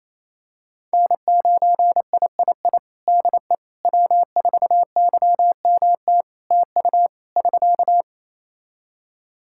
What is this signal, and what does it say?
Morse code, heard as a keyed tone, about 28 wpm, 700 Hz. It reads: N9III DE W4YMT TU <SK>
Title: Practice QSO - Exchange 4 - Ending - 28wpm